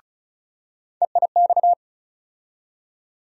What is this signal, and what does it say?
Morse code, heard as a keyed tone, about 36 wpm, 700 Hz. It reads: EIX
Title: USA Call Sign Postfixes - 3 Letters 35wpm